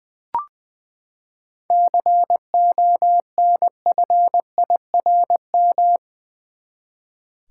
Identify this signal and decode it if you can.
Morse code, heard as a keyed tone, about 20 wpm, 700 Hz. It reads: CONFIRM
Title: Seven Letter Words - 20wpm